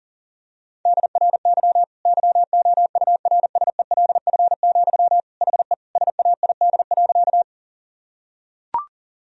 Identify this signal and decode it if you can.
Morse code, heard as a keyed tone, about 40 wpm, 700 Hz. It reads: DRY YOURSELF, HE SAID.
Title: Sentences from Top 1000 Words - Review Part 3 - 40wpm